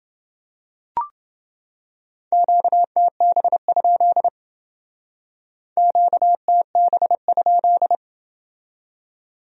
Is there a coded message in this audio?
Yes: Morse code from 2.3 to 8.0 s, heard as a keyed tone, about 29 wpm, 700 Hz. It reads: QTB? QTB?